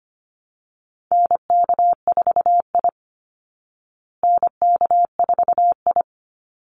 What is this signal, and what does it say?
Morse code, heard as a keyed tone, about 25 wpm, 700 Hz. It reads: NK4I NK4I
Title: Call Signs Repeated 2x - 25wpm